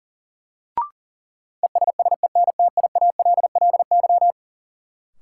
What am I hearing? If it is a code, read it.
Morse code, heard as a keyed tone, about 40 wpm, 700 Hz. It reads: ESSENTIALLY
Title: Sentences from Top 1300 Words - New Word Drill - 40wpm